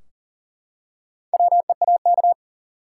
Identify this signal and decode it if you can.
Morse code, heard as a keyed tone, about 40 wpm, 700 Hz. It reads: WEAK
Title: Four Letter Words - 40wpm